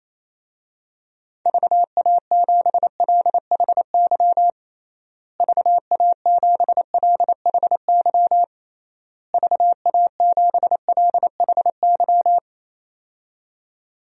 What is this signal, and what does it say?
Morse code, heard as a keyed tone, about 28 wpm, 700 Hz. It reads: VA7LHY VA7LHY VA7LHY
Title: DXCC-100 Call Signs - 6 Characters - Repeated 3x - 28wpm